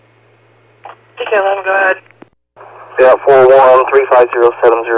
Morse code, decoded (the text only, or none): none